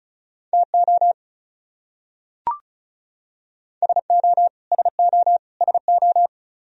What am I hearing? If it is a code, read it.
Morse code, heard as a keyed tone, about 34 wpm, 700 Hz. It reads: TO SO SO SO
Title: Top 100 Words Repeated 35wpm